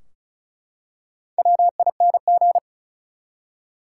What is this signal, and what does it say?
Morse code, heard as a keyed tone, about 36 wpm, 700 Hz. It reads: WING
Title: Four Letter Words - 35wpm